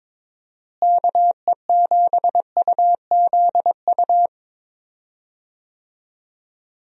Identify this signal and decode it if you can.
Morse code, heard as a keyed tone, about 21 wpm, 700 Hz. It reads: KE7UZU
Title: USA Call Signs - 6 Characters - Repeated 3x - 22wpm